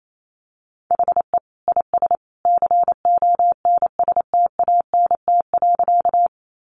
Morse code, heard as a keyed tone, about 28 wpm, 700 Hz. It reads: HE IS CONSTANT.
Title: Sentences from Top 1500 Words 28wpm